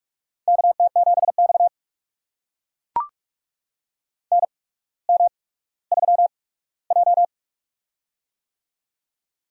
Morse code, heard as a keyed tone, about 44 wpm, spaced as for 19 wpm, 700 Hz. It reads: KT7X NM3J
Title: USA Call Signs - 4 Characters - 8x Character Spacing - 45wpm